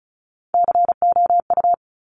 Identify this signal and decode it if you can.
Morse code, heard as a keyed tone, about 36 wpm, 700 Hz. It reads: COU
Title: Top Three Letter Combinations in Words 35wpm